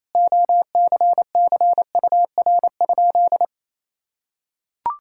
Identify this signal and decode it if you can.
Morse code, heard as a keyed tone, about 28 wpm, 700 Hz. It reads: OCCUR?
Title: Sentences from Top 1000 Words - Review Part 6 - 28wpm